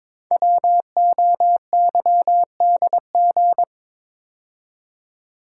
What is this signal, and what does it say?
Morse code, heard as a keyed tone, about 22 wpm, 700 Hz. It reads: WOYDG
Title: Code Groups - 5 Letters - 22wpm